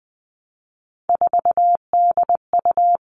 Morse code, heard as a keyed tone, about 20 wpm, 700 Hz. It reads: 4DU
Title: All Cut Numbers - 3-Digit 20wpm